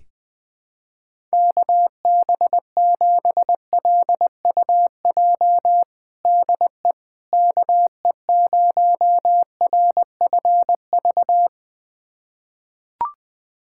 Morse code, heard as a keyed tone, about 20 wpm, 700 Hz. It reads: KB7LUJ DE KE0RFV
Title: USA Callsign DE Callsign - 6 Characters - 20wpm